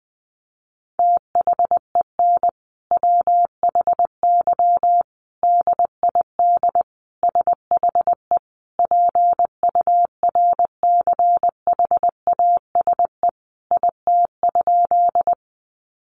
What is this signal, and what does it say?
Morse code, heard as a keyed tone, about 20 wpm, 700 Hz. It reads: THEN WHY DID SHE PURCHASE IT?
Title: Sentences from Top 800 Words 20wpm